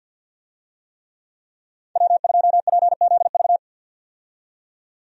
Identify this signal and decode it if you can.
Morse code, heard as a keyed tone, about 46 wpm, 700 Hz. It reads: W2PZV
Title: USA Call Signs - 5 Characters - 50wpm